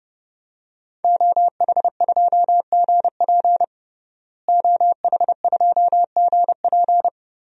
Morse code, heard as a keyed tone, about 32 wpm, 700 Hz. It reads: OH2GP OH2GP
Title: DXCC-100 Call Signs - 5 Characters - Repeated 2x - 30wpm